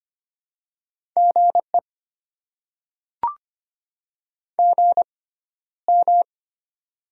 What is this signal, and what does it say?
Morse code, heard as a keyed tone, about 25 wpm, spaced as for 13 wpm, 700 Hz. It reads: GE GM
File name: Top 20 QSO Elements - 6x Character Spacing - 25wpm